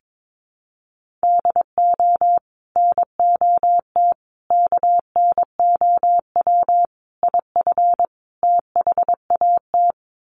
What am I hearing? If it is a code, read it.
Morse code, heard as a keyed tone, about 22 wpm, 700 Hz. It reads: DO NOT KNOW IF THAT
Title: Sets of 5 Words Encore2 22wpm